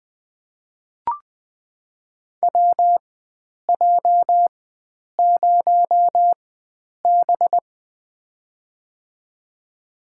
Morse code, heard as a keyed tone, about 20 wpm, spaced as for 9 wpm, 700 Hz. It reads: WJ0B
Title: USA Call Signs - 4x Character Spacing - 20wpm